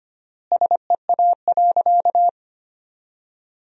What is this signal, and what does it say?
Morse code, heard as a keyed tone, about 25 wpm, 700 Hz. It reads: SEA.
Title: 4x Word-Spacing - Sentences from Top 300 Words - 25wpm